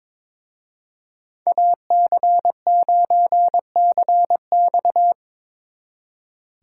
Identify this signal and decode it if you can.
Morse code, heard as a keyed tone, about 25 wpm, 700 Hz. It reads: AC9CX
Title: Speed Racing - USA 2x2 Call Signs 22wpm